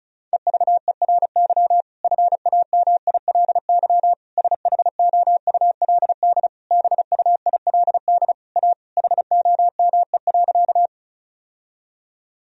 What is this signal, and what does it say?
Morse code, heard as a keyed tone, about 36 wpm, 700 Hz. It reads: EVERY FAMILY SHOULD BUILD A HOME.
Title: Sentences from Top 1000 Words - Review Part 1 - 35wpm